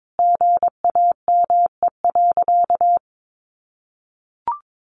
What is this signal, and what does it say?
Morse code, heard as a keyed tone, about 22 wpm, 700 Hz. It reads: GAME.
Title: Sentences from Top 1500 Words 22wpm